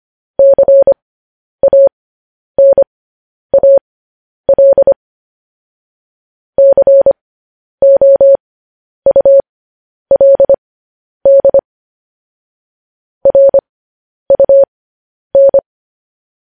Morse code, heard as a keyed tone, about 25 wpm, spaced as for 10 wpm, 550 Hz. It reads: CANAL COULD RUN
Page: news-headlines-in-morse-code-at-10-wpm